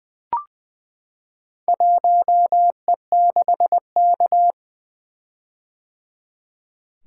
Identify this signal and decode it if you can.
Morse code, heard as a keyed tone, about 20 wpm, 700 Hz. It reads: 1E6K
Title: Four Letter-Number Combinations - 20wpm